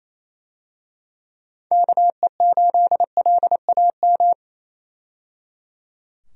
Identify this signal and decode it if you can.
Morse code, heard as a keyed tone, about 28 wpm, 700 Hz. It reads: KE8LAM
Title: USA Call Signs - 6 Characters - 28wpm